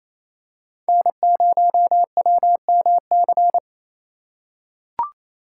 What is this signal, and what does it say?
Morse code, heard as a keyed tone, about 28 wpm, 700 Hz. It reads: N0WMC
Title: USA Call Signs - 5 Characters - 4x Character Spacing - 28wpm